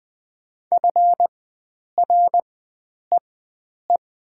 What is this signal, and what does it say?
Morse code, heard as a keyed tone, about 20 wpm, spaced as for 9 wpm, 700 Hz. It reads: FREE
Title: 4x Character Spacing - Top 500 Words - 20wpm